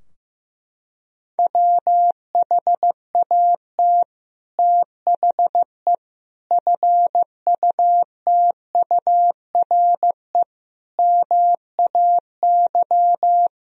Morse code, heard as a keyed tone, about 15 wpm, 700 Hz. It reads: WHAT THE FUTURE MAY
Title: Sentences from Top 1000 Words - Review Part 2 - 15wpm